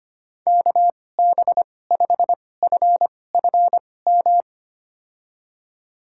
Morse code, heard as a keyed tone, about 25 wpm, spaced as for 18 wpm, 700 Hz. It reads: KB5FFM
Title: USA Call Signs - 6 Characters - 2x Character Spacing - 25wpm